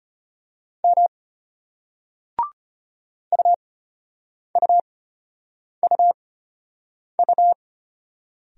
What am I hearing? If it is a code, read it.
Morse code, heard as a keyed tone, about 32 wpm, spaced as for 8 wpm, 700 Hz. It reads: M UUUU